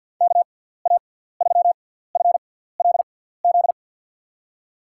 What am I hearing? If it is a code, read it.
Morse code, heard as a keyed tone, about 47 wpm, spaced as for 17 wpm, 700 Hz. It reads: KA3FLZ
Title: USA Call Signs - 6x Character Spacing - 50wpm